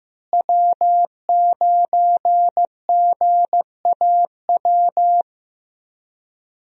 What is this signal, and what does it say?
Morse code, heard as a keyed tone, about 15 wpm, 700 Hz. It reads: W9GAW
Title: USA Call Signs - 5 Characters - Repeated 2x - 15wpm